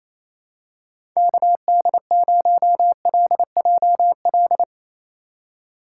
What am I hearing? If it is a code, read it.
Morse code, heard as a keyed tone, about 28 wpm, 700 Hz. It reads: KD0LJL
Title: USA Call Signs - 6 Characters - 2x Character Spacing - 28wpm